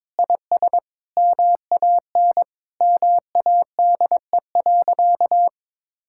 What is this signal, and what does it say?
Morse code, heard as a keyed tone, about 22 wpm, 700 Hz. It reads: IS MAN MADE.
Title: Sentences from Top 1000 Words - Review Part 6 - 22wpm